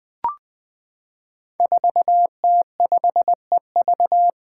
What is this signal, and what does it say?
Morse code, heard as a keyed tone, about 20 wpm, 700 Hz. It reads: 4T5EV
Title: Five Challenging Letter-Number Combinations 20wpm